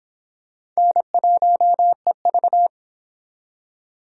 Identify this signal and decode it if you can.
Morse code, heard as a keyed tone, about 25 wpm, 700 Hz. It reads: N1EV